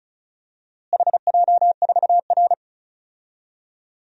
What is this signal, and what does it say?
Morse code, heard as a keyed tone, about 36 wpm, 700 Hz. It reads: HJ4R
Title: DXCC-100 Call Signs - 4 Characters - Repeated 2x - 35wpm